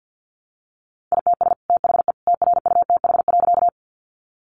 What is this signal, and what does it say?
Morse code, heard as a keyed tone, about 50 wpm, 700 Hz. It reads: ITS THE TRUTH.
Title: Sentences from Top 800 Words 50wpm